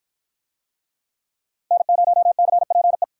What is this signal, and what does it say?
Morse code, heard as a keyed tone, about 53 wpm, 700 Hz. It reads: N0CPE